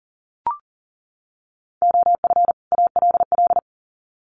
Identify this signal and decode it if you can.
Morse code, heard as a keyed tone, about 40 wpm, 700 Hz. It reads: OF ALL
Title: Sets of 2 Words 40wpm